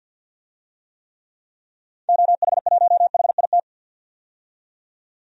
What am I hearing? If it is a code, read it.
Morse code, heard as a keyed tone, about 52 wpm, 700 Hz. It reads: OH1HIT
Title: DXCC-100 Call Signs - 6 Characters - 50wpm